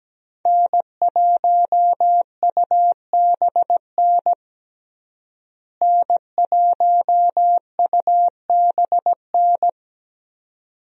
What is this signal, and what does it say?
Morse code, heard as a keyed tone, about 17 wpm, 700 Hz. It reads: N1UBN N1UBN
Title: USA Call Signs - 5 Characters - Repeated 2x - 17wpm